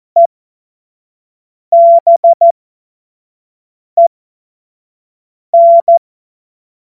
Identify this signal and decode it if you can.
Morse code, heard as a keyed tone, about 14 wpm, spaced as for 5 wpm, 680 Hz. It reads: EBEN
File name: Witz-12-August-2025-25-F70